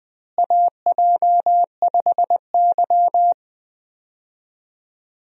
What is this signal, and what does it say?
Morse code, heard as a keyed tone, about 20 wpm, 700 Hz. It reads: AJ5Y
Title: USA Call Signs - 4 Characters - Repeated 3x - 20wpm